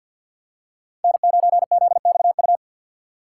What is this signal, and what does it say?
Morse code, heard as a keyed tone, about 50 wpm, 700 Hz. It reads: N9ZXU
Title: USA Call Signs - 5 Characters - 3x Character Spacing - 50wpm